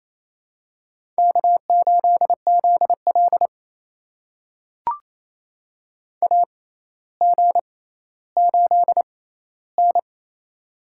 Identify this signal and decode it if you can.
Morse code, heard as a keyed tone, about 28 wpm, spaced as for 14 wpm, 700 Hz. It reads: K8ZL AG8N
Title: USA Call Signs - 4 Characters - 6x Character Spacing - 28wpm